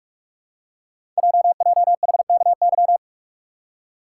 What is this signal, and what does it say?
Morse code, heard as a keyed tone, about 44 wpm, 700 Hz. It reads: JJHKY
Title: Code Groups - 5 Letters - 45wpm